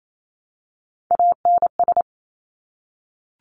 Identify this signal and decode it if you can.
Morse code, heard as a keyed tone, about 28 wpm, 700 Hz. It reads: ANS
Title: Top Three Letter Combinations in Words 28wpm